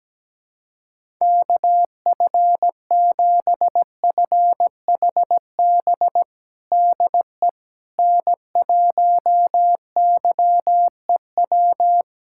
Morse code, heard as a keyed tone, about 17 wpm, 700 Hz. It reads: KF7FHB DE N1YEW
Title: USA Callsign DE Callsign 17wpm